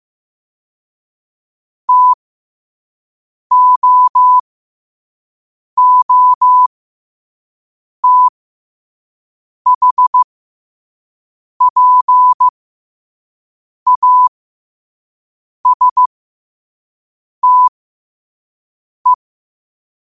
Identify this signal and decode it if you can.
Morse code, heard as a keyed tone, about 15 wpm, spaced as for 5 wpm, 1000 Hz. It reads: TOOTHPASTE
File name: qotdfiles / 5wpm